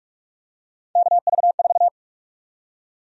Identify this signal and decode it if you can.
Morse code, heard as a keyed tone, about 46 wpm, 700 Hz. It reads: KV4
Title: USA Call Sign Prefixes - 45wpm